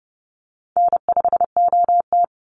Morse code, heard as a keyed tone, about 30 wpm, 700 Hz. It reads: N5OT
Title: Easy Call Signs - 30wpm